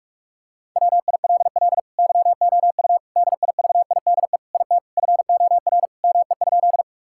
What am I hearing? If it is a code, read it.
Morse code, heard as a keyed tone, about 46 wpm, 700 Hz. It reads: WILL YOU DIVIDE IT FOR ME?
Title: Sentences from Top 1600 Words 45wpm